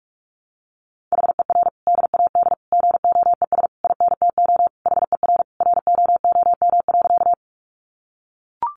Morse code, heard as a keyed tone, about 44 wpm, 700 Hz. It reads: HER DAD GOES INTO HER ROOM.
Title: Additional Easy Phrases 45wpm